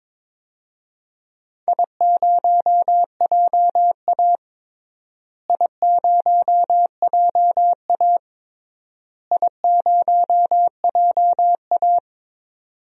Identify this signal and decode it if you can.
Morse code, heard as a keyed tone, about 22 wpm, 700 Hz. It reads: I0JA I0JA I0JA
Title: DXCC-100 Call Signs - 4 Characters - Repeated 3x - 22wpm